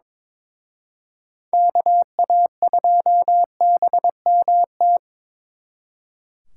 Morse code, heard as a keyed tone, about 27 wpm, 700 Hz. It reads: KA2BMT
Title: Speed Racing - USA 2x3 Call Signs 22wpm